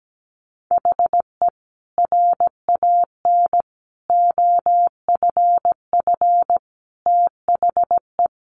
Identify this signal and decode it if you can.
Morse code, heard as a keyed tone, about 17 wpm, 700 Hz. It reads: HE RAN OFF THE
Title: A Sound Of Thunder 17wpm